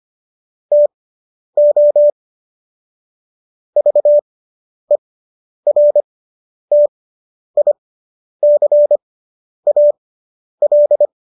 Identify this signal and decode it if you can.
Morse code, heard as a keyed tone, about 25 wpm, spaced as for 10 wpm, 590 Hz. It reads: TO VERTICAL